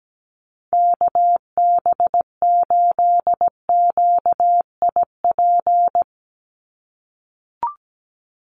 Morse code, heard as a keyed tone, about 17 wpm, 700 Hz. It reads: KB8QIP
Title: USA Call Signs 17wpm